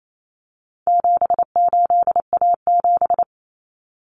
Morse code, heard as a keyed tone, about 28 wpm, 700 Hz. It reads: 78A7
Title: Common Cut Numbers - 4-Digit 28wpm